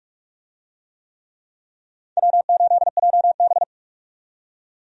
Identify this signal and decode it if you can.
Morse code, heard as a keyed tone, about 45 wpm, 700 Hz. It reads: W8JB
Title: USA Call Signs - 4 Characters - 45wpm